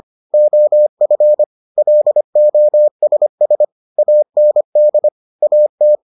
Morse code, heard as a keyed tone, about 25 wpm, 600 Hz. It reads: OF LOSS AND AT